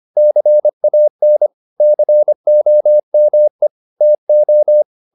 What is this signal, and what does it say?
Morse code, heard as a keyed tone, about 25 wpm, 600 Hz. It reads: CAN COME TO
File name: the old man and the sea part 1 25wpm